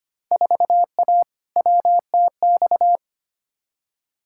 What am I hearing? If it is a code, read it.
Morse code, heard as a keyed tone, about 25 wpm, 700 Hz. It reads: 4A WTX
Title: Field Day Exchanges Encore - 25wpm